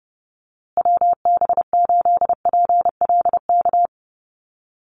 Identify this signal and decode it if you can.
Morse code, heard as a keyed tone, about 30 wpm, 700 Hz. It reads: WB8PLK
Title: USA Call Signs Repeated 2x 30wpm